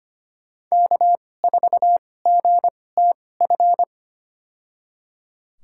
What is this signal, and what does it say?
Morse code, heard as a keyed tone, about 25 wpm, spaced as for 18 wpm, 700 Hz. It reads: K4GTF
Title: USA Call Signs - 5 Characters - 2x Character Spacing - 25wpm